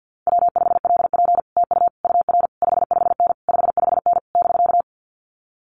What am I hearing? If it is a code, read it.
Morse code, heard as a keyed tone, about 50 wpm, 700 Hz. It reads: W5LP TU UR 55N 55N <BK>
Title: Common QSO Phrases Encore 50wpm